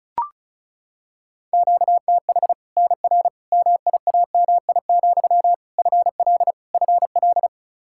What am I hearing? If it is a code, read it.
Morse code, heard as a keyed tone, about 36 wpm, 700 Hz. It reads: QTH NR MIAMI, FL FL
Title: Practice QSO - Exchange 1 - Beginning - 35wpm